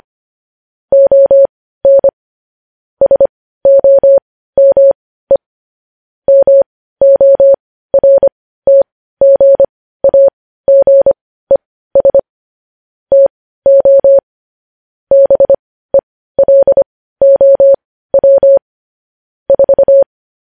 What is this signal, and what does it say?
Morse code, heard as a keyed tone, about 25 wpm, spaced as for 15 wpm, 550 Hz. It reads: ON SOME MORTGAGES TO BELOW 4
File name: podcast / News headlines-1x-25x15